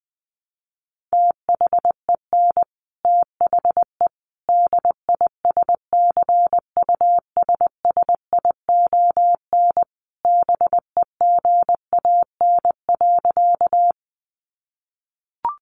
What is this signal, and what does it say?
Morse code, heard as a keyed tone, about 20 wpm, 700 Hz. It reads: THEN THE DISCUSSION BEGAN.
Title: Sentences from Top 1100 Words 20wpm